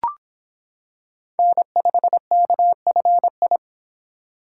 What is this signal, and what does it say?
Morse code, heard as a keyed tone, about 26 wpm, 700 Hz. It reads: N5KFI